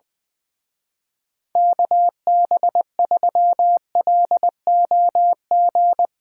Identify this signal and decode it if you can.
Morse code, heard as a keyed tone, about 21 wpm, 700 Hz. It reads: KB3LOG
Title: Speed Racing - USA 2x3 Call Signs 20wpm